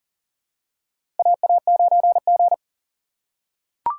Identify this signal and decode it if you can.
Morse code, heard as a keyed tone, about 37 wpm, 700 Hz. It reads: AA9G
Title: USA Call Signs - 6x Character Spacing - 40wpm